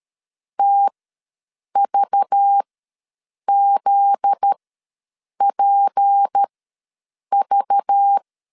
Morse code, heard as a keyed tone, about 13 wpm, spaced as for 7 wpm, 800 Hz. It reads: TVZPV
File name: 7 GF9Z ws 12wpm